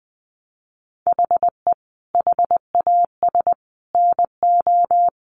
Morse code, heard as a keyed tone, about 20 wpm, 700 Hz. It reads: HE HAS NO
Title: Sentences from Top 1500 Words 20wpm